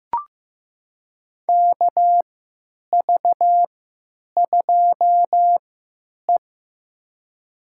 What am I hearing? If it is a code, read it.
Morse code, heard as a keyed tone, about 15 wpm, spaced as for 9 wpm, 700 Hz. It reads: KV2E
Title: USA Call Signs - 4 Characters - 3x Character Spacing - 15wpm